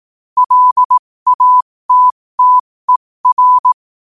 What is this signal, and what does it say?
Morse code, heard as a keyed tone, about 18 wpm, spaced as for 15 wpm, 1000 Hz. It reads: LATTER
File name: qotdfiles / 15wpm